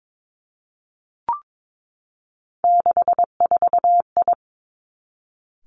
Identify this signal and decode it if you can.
Morse code, heard as a keyed tone, about 22 wpm, 700 Hz. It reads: 64I
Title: Three Challenging Letter-Number Combinations 22wpm